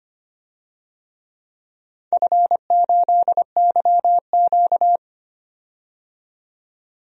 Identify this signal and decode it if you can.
Morse code, heard as a keyed tone, about 25 wpm, 700 Hz. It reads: F8YQ
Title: DXCC-100 Call Signs - 4 Characters - 25wpm